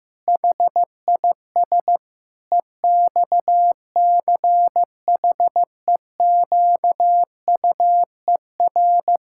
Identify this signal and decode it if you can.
Morse code, heard as a keyed tone, about 15 wpm, 700 Hz. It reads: HIS EXCHEQUER
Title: Pangram Challenge v2 - 15wpm